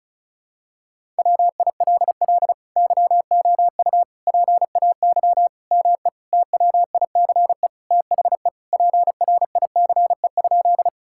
Morse code, heard as a keyed tone, about 36 wpm, 700 Hz. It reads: WILL YOU PAY ME TWICE THE PRICE?
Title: Sentences from Top 1600 Words 35wpm